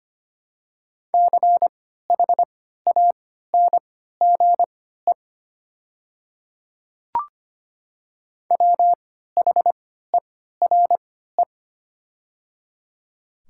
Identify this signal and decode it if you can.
Morse code, heard as a keyed tone, about 25 wpm, spaced as for 14 wpm, 700 Hz. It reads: CHANGE WHERE
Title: 3x Character Spacing - Top 300 Words - 25wpm